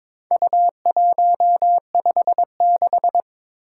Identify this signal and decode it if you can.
Morse code, heard as a keyed tone, about 23 wpm, 700 Hz. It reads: U156
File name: practice / 22wpm / Four Letter-Number Combinations - 22wpm